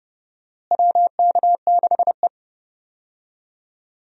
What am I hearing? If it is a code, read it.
Morse code, heard as a keyed tone, about 29 wpm, 700 Hz. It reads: WK6E
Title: USA Call Signs - 4 Characters - Repeated 3x - 30wpm